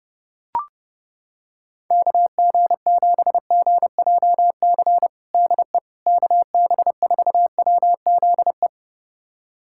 Morse code, heard as a keyed tone, about 30 wpm, 700 Hz. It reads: KG7GJC DE KB4WZE
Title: USA Callsign DE Callsign 30wpm